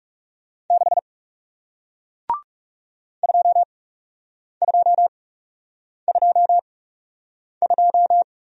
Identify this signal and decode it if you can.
Morse code, heard as a keyed tone, about 40 wpm, spaced as for 8 wpm, 700 Hz. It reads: 6 2222